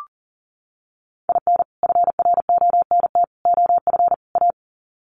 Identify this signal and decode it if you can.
Morse code, heard as a keyed tone, about 40 wpm, 700 Hz. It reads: IN FRONT OF A
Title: Sets of 4 Words 40wpm